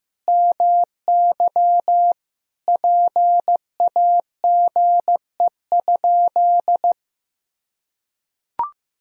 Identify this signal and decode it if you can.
Morse code, heard as a keyed tone, about 15 wpm, 700 Hz. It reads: MY PAGE?
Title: Sentences from Top 1000 Words - Review Part 1 - 15wpm